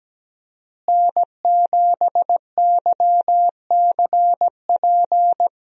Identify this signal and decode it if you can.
Morse code, heard as a keyed tone, about 17 wpm, 700 Hz. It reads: N7YCP
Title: USA Call Signs - 5 Characters - Repeated 2x - 17wpm